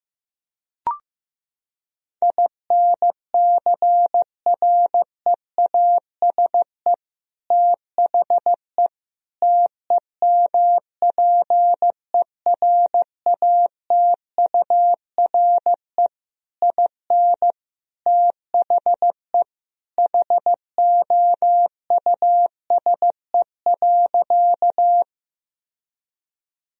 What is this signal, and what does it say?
Morse code, heard as a keyed tone, about 15 wpm, 700 Hz. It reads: INCREASE THE TEMPERATURE IN THE HOUSE.
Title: Sentences from Top 1000 Words - Review Part 4 - 15wpm